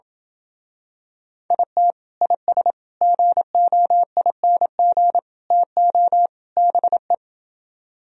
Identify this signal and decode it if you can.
Morse code, heard as a keyed tone, about 28 wpm, 700 Hz. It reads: IT IS GOING TO BE